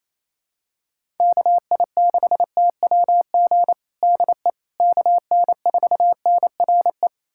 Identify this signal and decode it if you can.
Morse code, heard as a keyed tone, about 28 wpm, 700 Hz. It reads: KI6TWG DE KN4NRE
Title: USA Callsign DE Callsign - 6 Characters - 28wpm